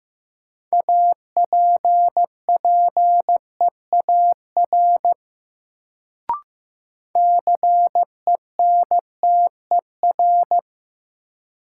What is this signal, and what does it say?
Morse code, heard as a keyed tone, about 15 wpm, 700 Hz. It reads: APPEAR CENTER
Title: Sentences from Top 1000 Words - Review Part 2 - New Word Drill - 15wpm